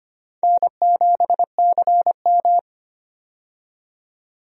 Morse code, heard as a keyed tone, about 25 wpm, 700 Hz. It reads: N7CM
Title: USA Call Signs - 4 Characters - 25wpm